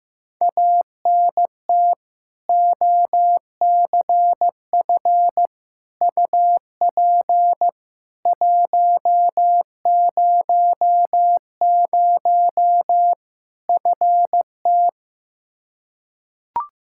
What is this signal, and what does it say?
Morse code, heard as a keyed tone, about 15 wpm, 700 Hz. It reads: ANT OCF UP 100 FT
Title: K1USN SST QSO Practice 15wpm